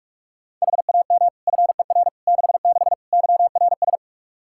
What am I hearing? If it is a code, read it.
Morse code, heard as a keyed tone, about 46 wpm, 700 Hz. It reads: HAM FER 66 YRS
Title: Practice QSO - Exchange 3 - More Info - 45wpm